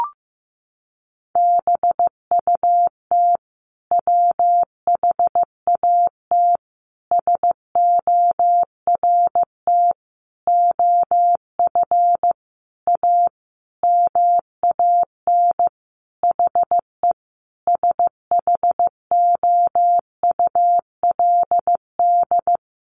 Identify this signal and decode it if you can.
Morse code, heard as a keyed tone, about 15 wpm, 700 Hz. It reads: BUT WHAT SORT OF A MAN HE SHOULD
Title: Sentences from Top 1300 Words 15wpm